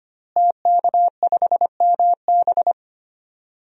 Morse code, heard as a keyed tone, about 25 wpm, 700 Hz. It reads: TK5MB
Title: DXCC-100 Call Signs - 5 Characters - Repeated 3x - 25wpm